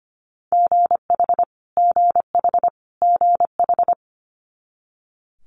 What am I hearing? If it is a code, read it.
Morse code, heard as a keyed tone, about 25 wpm, 700 Hz. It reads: GH GH GH
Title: Multi-letter Phonemes Repeated 3x 25wpm